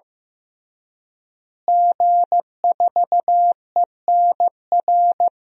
Morse code, heard as a keyed tone, about 15 wpm, 700 Hz. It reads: G4ENR
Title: DXCC-100 Call Signs - 5 Characters - Repeated 3x - 15wpm